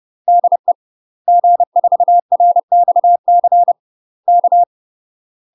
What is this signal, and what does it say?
Morse code, heard as a keyed tone, about 30 wpm, 700 Hz. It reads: DE G4RXC K
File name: UK MASTER QSOs TRAINING / 30WPM / UK MASTER QSOs 0007